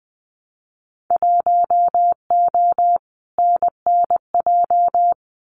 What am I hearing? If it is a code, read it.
Morse code, heard as a keyed tone, about 19 wpm, 700 Hz. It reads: 1O NNJ
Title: Winter Field Day Exchanges 20wpm